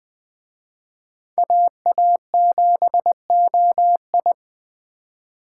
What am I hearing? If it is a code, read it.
Morse code, heard as a keyed tone, about 20 wpm, 700 Hz. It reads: AA7OI
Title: Speed Racing - USA 2x2 Call Signs 17wpm